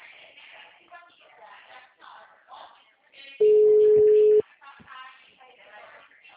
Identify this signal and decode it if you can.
Ringback tone (Europe, 425 Hz), with speech.